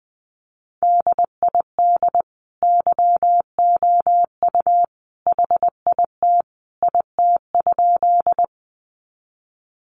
Morse code, heard as a keyed tone, about 19 wpm, 700 Hz. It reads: DID YOU HIT IT?
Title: Sentences from Top 1200 Words 20wpm